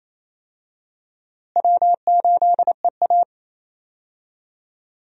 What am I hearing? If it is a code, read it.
Morse code, heard as a keyed tone, about 28 wpm, 700 Hz. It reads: W8EA
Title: USA Call Signs - 4 Characters - Repeated 3x - 28wpm